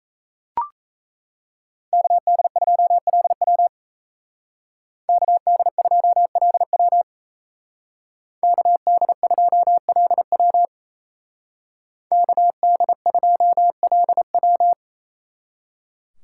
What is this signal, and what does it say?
Morse code, heard as a keyed tone, about 34 wpm, 700 Hz. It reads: KD2LW KD2LW KD2LW KD2LW
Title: Speed Racing - USA Call Signs 28wpm